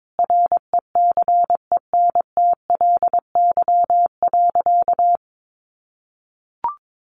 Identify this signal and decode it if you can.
Morse code, heard as a keyed tone, about 22 wpm, 700 Hz. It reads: RECENTLY.
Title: Sentences from Top 1200 Words 22wpm